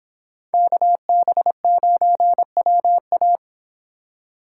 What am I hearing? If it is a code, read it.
Morse code, heard as a keyed tone, about 30 wpm, 700 Hz. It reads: KB9WA